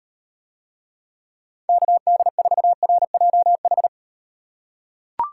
Morse code, heard as a keyed tone, about 36 wpm, 700 Hz. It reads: KD4RJH